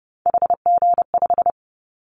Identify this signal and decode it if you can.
Morse code, heard as a keyed tone, about 30 wpm, 700 Hz. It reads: HG5
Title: Callsign Prefixes for Top 50 DXCC - 30wpm